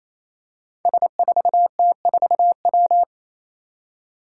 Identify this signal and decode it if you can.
Morse code, heard as a keyed tone, about 28 wpm, 700 Hz. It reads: S4T4W
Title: Five Challenging Letter-Number Combinations 28wpm